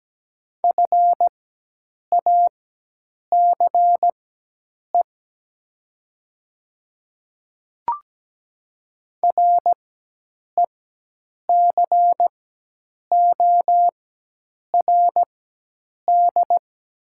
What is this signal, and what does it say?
Morse code, heard as a keyed tone, about 17 wpm, spaced as for 8 wpm, 700 Hz. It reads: FACE RECORD
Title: 4x Character Spacing - Top 500 Words - 17wpm